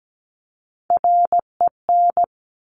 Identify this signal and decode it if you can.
Morse code, heard as a keyed tone, about 17 wpm, 700 Hz. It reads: REN
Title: Top Three Letter Combinations in Words 17wpm